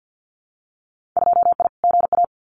Morse code, heard as a keyed tone, about 50 wpm, 700 Hz. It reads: 2I GA
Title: Winter Field Day Exchanges 50wpm